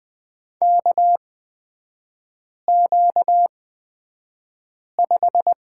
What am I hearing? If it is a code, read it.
Morse code, heard as a keyed tone, about 20 wpm, spaced as for 5 wpm, 690 Hz. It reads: KQ5
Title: 236 Lesson 38e - Letter q - Three Characters 20wpm